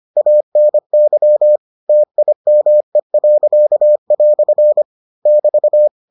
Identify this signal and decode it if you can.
Morse code, heard as a keyed tone, about 25 wpm, 600 Hz. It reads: ANY TIME." <BT>